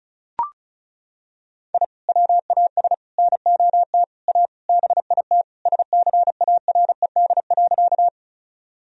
Morse code, heard as a keyed tone, about 36 wpm, 700 Hz. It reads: I WAS NOT A BIT SCARED.
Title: Sentences from Top 1000 Words - Review Part 4 - 35wpm